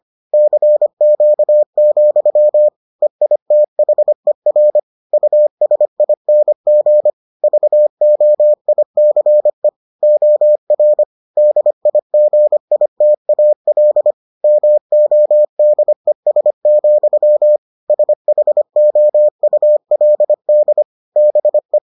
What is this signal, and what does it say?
Morse code, heard as a keyed tone, about 25 wpm, 600 Hz. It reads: CQ, EITHER USING VOICE OR DIGITAL MODES, SHOULD BE